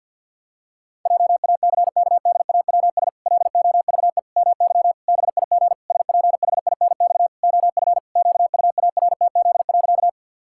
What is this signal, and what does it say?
Morse code, heard as a keyed tone, about 46 wpm, 700 Hz. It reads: JACKDAWS LOVE MY BIG SPHINX OF QUARTZ.
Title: Pangram Challenge v2 - 50wpm